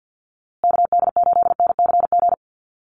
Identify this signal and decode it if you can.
Morse code, heard as a keyed tone, about 48 wpm, 700 Hz. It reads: KD8NCG
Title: USA Call Signs Repeated 3x 50wpm